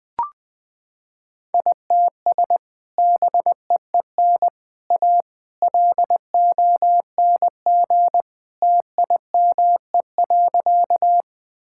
Morse code, heard as a keyed tone, about 21 wpm, 700 Hz. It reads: ITS BEEN A LONG TIME.
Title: Sentences with 3-Word Phrases - 20wpm